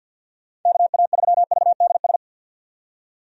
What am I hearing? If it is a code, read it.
Morse code, heard as a keyed tone, about 50 wpm, 700 Hz. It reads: KA3VDS
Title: USA Call Signs - 6 Characters - 50wpm